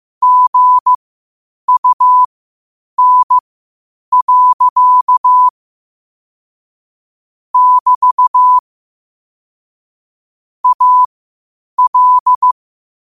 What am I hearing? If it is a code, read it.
Morse code, heard as a keyed tone, about 15 wpm, spaced as for 8 wpm, 1000 Hz. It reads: GUN. <BT> AL
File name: qotdfiles / 8wpm